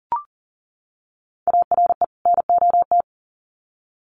Morse code, heard as a keyed tone, about 40 wpm, 700 Hz. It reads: ARE NOT
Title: Sets of 2 Words 40wpm